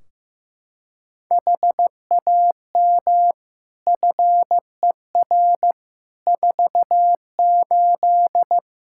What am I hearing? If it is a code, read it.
Morse code, heard as a keyed tone, about 15 wpm, 700 Hz. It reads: HAM FER 48
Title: Practice QSO - Exchange 3 - More Info - 15wpm